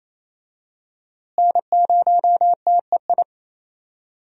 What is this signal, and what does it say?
Morse code, heard as a keyed tone, about 28 wpm, 700 Hz. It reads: N0TEI
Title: USA Call Signs - 5 Characters - Repeated 3x - 28wpm